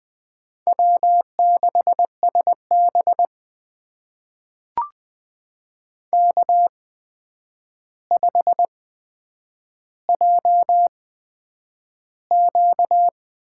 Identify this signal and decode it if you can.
Morse code, heard as a keyed tone, about 20 wpm, spaced as for 9 wpm, 700 Hz. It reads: W6SB K5JQ
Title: USA Call Signs - 4 Characters - 8x Character Spacing - 20wpm